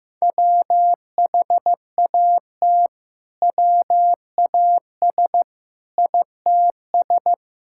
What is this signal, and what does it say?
Morse code, heard as a keyed tone, about 15 wpm, 700 Hz. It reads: WHAT WAS ITS
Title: Sentences from Top 1000 Words - Review Part 5 - 15wpm